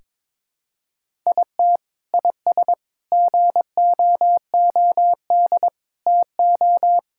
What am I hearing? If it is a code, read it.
Morse code, heard as a keyed tone, about 22 wpm, 700 Hz. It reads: IT IS GOOD TO
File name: practice / 22wpm / Sentences with 3-Word Phrases - 22wpm